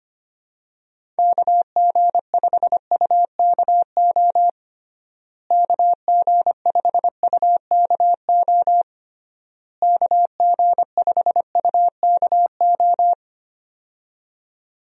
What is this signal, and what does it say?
Morse code, heard as a keyed tone, about 25 wpm, 700 Hz. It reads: KG5UKO KG5UKO KG5UKO
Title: USA Call Signs - 6 Characters - Repeated 3x - 25wpm